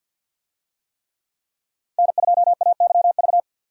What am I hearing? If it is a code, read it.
Morse code, heard as a keyed tone, about 53 wpm, 700 Hz. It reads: N2AYV
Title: USA Call Signs - 5 Characters - 50wpm